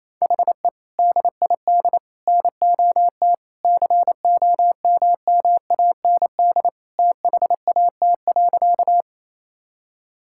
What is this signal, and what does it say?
Morse code, heard as a keyed tone, about 28 wpm, 700 Hz. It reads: HE DID NOT COMMAND THAT.
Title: Sentences from Top 1600 Words 28wpm